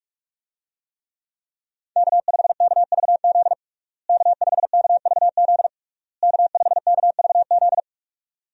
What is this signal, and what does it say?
Morse code, heard as a keyed tone, about 46 wpm, 700 Hz. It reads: K5KVZ K5KVZ K5KVZ
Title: USA Call Signs - 5 Characters - Repeated 3x - 45wpm